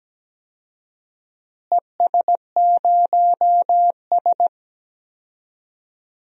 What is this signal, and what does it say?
Morse code, heard as a keyed tone, about 17 wpm, 700 Hz. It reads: ES0S
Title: DXCC-100 Call Signs - 4 Characters - Repeated 3x - 17wpm